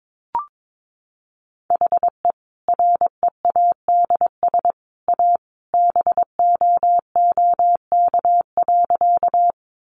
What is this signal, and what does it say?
Morse code, heard as a keyed tone, about 22 wpm, 700 Hz. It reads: HE READS A BOOK.
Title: Additional Easy Phrases 22wpm